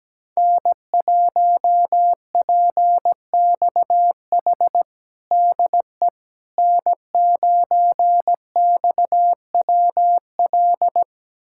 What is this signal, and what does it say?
Morse code, heard as a keyed tone, about 17 wpm, 700 Hz. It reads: N1PXH DE N9XWL
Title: USA Callsign DE Callsign - 5 Characters - 17wpm